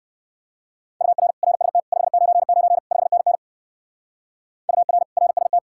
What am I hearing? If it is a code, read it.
Morse code, heard as a keyed tone, about 60 wpm, 700 Hz. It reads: UR RST 599 5NN UR RST